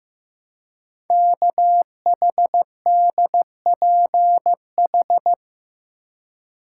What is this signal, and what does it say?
Morse code, heard as a keyed tone, about 15 wpm, 700 Hz. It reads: KHDPH
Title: Code Groups - 5 Letters - 15wpm